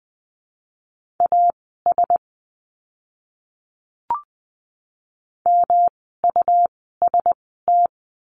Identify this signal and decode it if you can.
Morse code, heard as a keyed tone, about 20 wpm, spaced as for 14 wpm, 700 Hz. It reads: AS MUST
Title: 237 Lesson 38f - Letter q - Words 20wpm